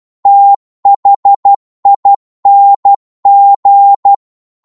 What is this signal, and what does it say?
Morse code, heard as a keyed tone, about 12 wpm, 800 Hz. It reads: THING